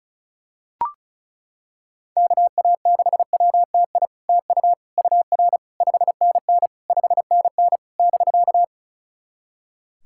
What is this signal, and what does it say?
Morse code, heard as a keyed tone, about 36 wpm, 700 Hz. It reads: KA6WTI TU UR 5NN 5NN <BK>
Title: SOTA QSO Practice 35wpm